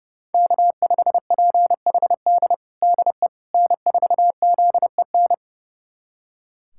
Morse code, heard as a keyed tone, about 30 wpm, 700 Hz. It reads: K5PHD DE N4ZEN
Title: USA Callsign DE Callsign - 5 Characters - 30wpm